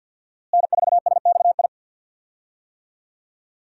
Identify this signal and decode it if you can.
Morse code, heard as a keyed tone, about 51 wpm, 700 Hz. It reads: N4SXI
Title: USA Call Signs - 5 Characters - Repeated 3x - 50wpm